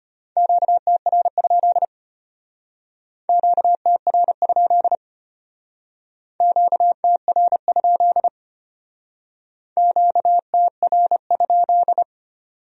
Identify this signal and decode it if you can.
Morse code, heard as a keyed tone, about 31 wpm, 700 Hz. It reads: QTR? QTR? QTR? QTR?
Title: Speed-Racing Q-Codes - 25wpm